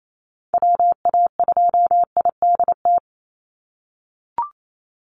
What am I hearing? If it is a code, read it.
Morse code, heard as a keyed tone, about 28 wpm, 700 Hz. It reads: WA2IDT
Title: USA Call Signs 28wpm